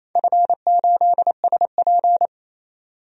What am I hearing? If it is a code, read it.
Morse code, heard as a keyed tone, about 28 wpm, 700 Hz. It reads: F8SP
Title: DXCC-100 Call Signs - 4 Characters - 28wpm